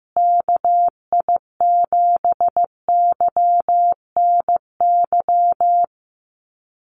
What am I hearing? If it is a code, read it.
Morse code, heard as a keyed tone, about 15 wpm, 700 Hz. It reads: KI7YNY
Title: USA Call Signs Repeated 3x 15wpm